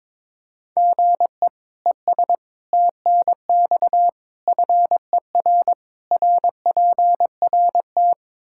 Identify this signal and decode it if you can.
Morse code, heard as a keyed tone, about 22 wpm, 700 Hz. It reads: GE ES TNX FER RPRT
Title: K1USN SST QSO Practice 22wpm